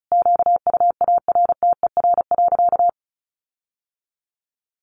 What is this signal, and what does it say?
Morse code, heard as a keyed tone, about 34 wpm, 700 Hz. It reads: QUARTER.
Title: Sentences from Top 1500 Words 35wpm